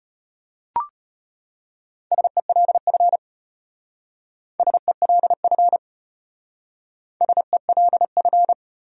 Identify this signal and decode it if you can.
Morse code, heard as a keyed tone, about 33 wpm, 700 Hz. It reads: SELF SELF SELF